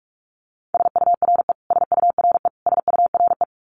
Morse code, heard as a keyed tone, about 44 wpm, 700 Hz. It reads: SURE SURE SURE
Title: Top 300 Words Repeated 45wpm